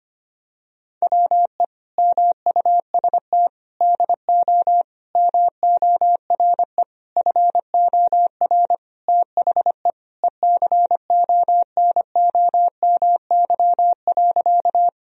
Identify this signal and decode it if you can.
Morse code, heard as a keyed tone, about 25 wpm, 700 Hz. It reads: WE MUST DO MORE FOR THE ECONOMY.
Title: Sentences from Top 1000 Words - Review Part 3 - 25wpm